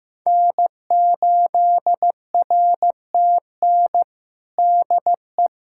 Morse code, heard as a keyed tone, about 15 wpm, 700 Hz. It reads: N8RTN DE
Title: USA Callsign DE Callsign - 5 Characters - 15wpm